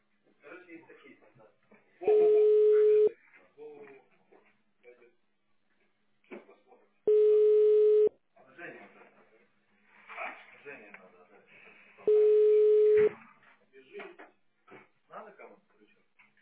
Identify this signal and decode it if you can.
Ringback tone (Europe, 425 Hz), with speech.